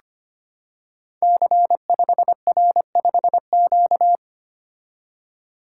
Morse code, heard as a keyed tone, about 25 wpm, 700 Hz. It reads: C5R5Q
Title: Code Groups - 5 Letter:Numbers - 25wpm